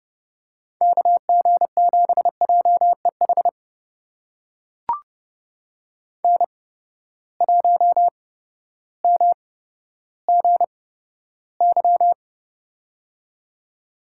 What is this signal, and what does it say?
Morse code, heard as a keyed tone, about 30 wpm, spaced as for 14 wpm, 700 Hz. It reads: KG7JEH N1MGY
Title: USA Call Signs - 8x Character Spacing - 30wpm